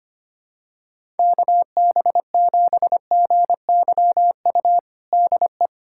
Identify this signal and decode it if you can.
Morse code, heard as a keyed tone, about 25 wpm, 700 Hz. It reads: KB7GYU DE
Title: USA Callsign DE Callsign - 6 Characters - 25wpm